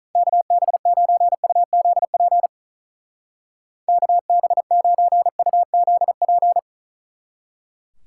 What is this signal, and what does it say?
Morse code, heard as a keyed tone, about 47 wpm, 700 Hz. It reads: KB9UZP KB9UZP